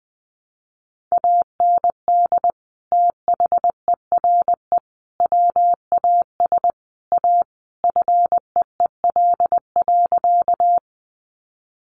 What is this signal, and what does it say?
Morse code, heard as a keyed tone, about 20 wpm, 700 Hz. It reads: AND THERE WAS A FEEL.
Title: A Sound Of Thunder 20wpm